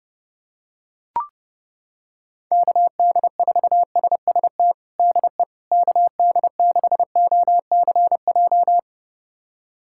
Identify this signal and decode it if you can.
Morse code, heard as a keyed tone, about 32 wpm, 700 Hz. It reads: KD4SST DE KD6OCJ
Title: USA Callsign DE Callsign - 6 Characters - 30wpm